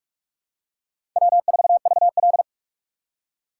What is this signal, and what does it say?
Morse code, heard as a keyed tone, about 46 wpm, 700 Hz. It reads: W4VL
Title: USA Call Signs - 4 Characters - 2x Character Spacing - 45wpm